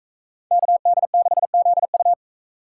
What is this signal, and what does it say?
Morse code, heard as a keyed tone, about 42 wpm, 700 Hz. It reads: KD6ZU